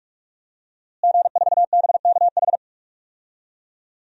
Morse code, heard as a keyed tone, about 46 wpm, 700 Hz. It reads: G4BKH
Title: DXCC-100 Call Signs - 5 Characters - Repeated 3x - 45wpm